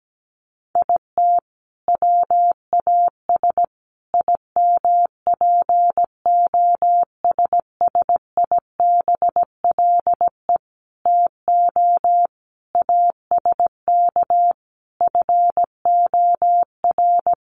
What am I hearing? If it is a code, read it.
Morse code, heard as a keyed tone, about 17 wpm, 700 Hz. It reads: IT WAS IMPOSSIBLE TO ASK FOR
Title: Sentences from Top 1100 Words 17wpm